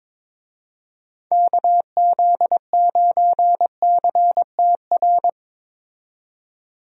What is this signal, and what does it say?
Morse code, heard as a keyed tone, about 22 wpm, 700 Hz. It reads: KZ9CTR
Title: USA Call Signs - 6 Characters - 22wpm